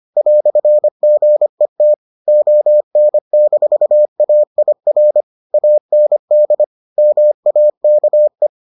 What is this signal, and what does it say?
Morse code, heard as a keyed tone, about 25 wpm, 600 Hz. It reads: "GET ON-AIR AND MAKE